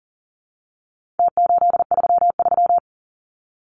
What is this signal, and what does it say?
Morse code, heard as a keyed tone, about 40 wpm, 700 Hz. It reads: T833
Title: Numbers - Contest Style - 4-Digit 40wpm